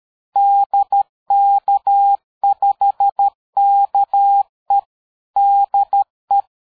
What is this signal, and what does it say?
Morse code, heard as a keyed tone, about 13 wpm, 800 Hz. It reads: DK5KE DE
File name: dk5ke / QSO 60BpM